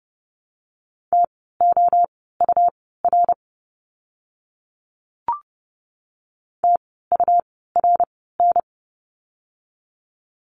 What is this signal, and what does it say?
Morse code, heard as a keyed tone, about 30 wpm, spaced as for 17 wpm, 700 Hz. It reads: TOUR TURN
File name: Lesson 17f - Letter u - Words 30wpm